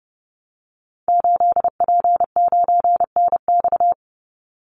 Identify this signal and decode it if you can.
Morse code, heard as a keyed tone, about 30 wpm, 700 Hz. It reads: 8P9NX
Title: Call Signs - 30wpm